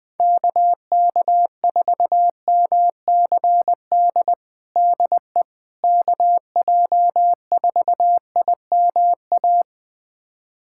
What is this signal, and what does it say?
Morse code, heard as a keyed tone, about 21 wpm, 700 Hz. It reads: KK4MCD DE KJ4IMA
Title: USA Callsign DE Callsign - 6 Characters - 20wpm